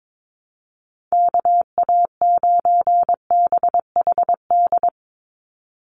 Morse code, heard as a keyed tone, about 22 wpm, 700 Hz. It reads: KA9BHD
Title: Call Signs - 22wpm